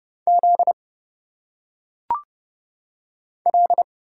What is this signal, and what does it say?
Morse code, heard as a keyed tone, about 30 wpm, 700 Hz. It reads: Z L
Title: Single Letters - Phonetic Pronunciation 30wpm